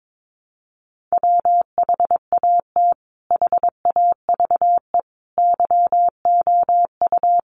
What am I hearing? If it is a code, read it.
Morse code, heard as a keyed tone, about 22 wpm, 700 Hz. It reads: WHAT HAVE YOU
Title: Sentences from Top 400 Words 22wpm